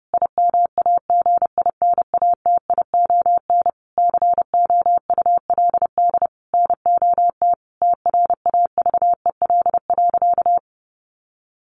Morse code, heard as a keyed tone, about 30 wpm, 700 Hz. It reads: IMAGINATION COULD NOT TRAVEL.
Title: Sentences from Top 1300 Words 30wpm